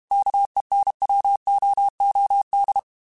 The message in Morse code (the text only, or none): KENWOOD